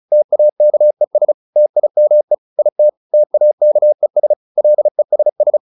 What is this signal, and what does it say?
Morse code, heard as a keyed tone, about 34 wpm, 600 Hz. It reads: TAKES TIME IT TAKES LESS